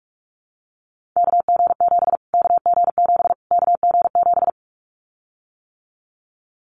Morse code, heard as a keyed tone, about 44 wpm, 700 Hz. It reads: KG7 KG7 KG7
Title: Callsign Prefixes for Top 25 DXCC - Repeated 3x - 45wpm